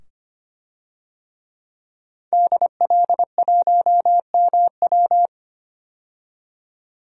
Morse code, heard as a keyed tone, about 25 wpm, 700 Hz. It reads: DL1MW
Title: DXCC-100 Call Signs - 5 Characters - 25wpm